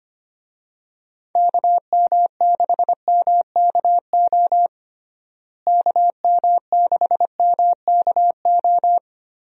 Morse code, heard as a keyed tone, about 25 wpm, 700 Hz. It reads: KM6MKO KM6MKO
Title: USA Call Signs - 6 Characters - Repeated 3x - 25wpm